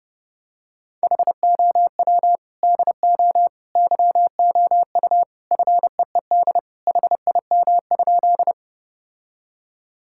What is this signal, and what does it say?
Morse code, heard as a keyed tone, about 30 wpm, 700 Hz. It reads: HOW DO YOU FEED HIM?
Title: Sentences from Top 1600 Words 30wpm